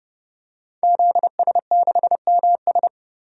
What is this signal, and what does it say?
Morse code, heard as a keyed tone, about 31 wpm, 700 Hz. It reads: ZS6MS
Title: DXCC-100 Call Signs - 5 Characters - 30wpm